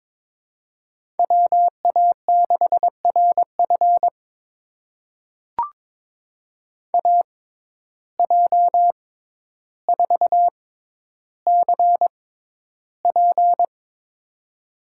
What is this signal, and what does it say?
Morse code, heard as a keyed tone, about 22 wpm, spaced as for 11 wpm, 700 Hz. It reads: WA6RF AJ4CP
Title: USA Call Signs - 5 Characters - 6x Character Spacing - 22wpm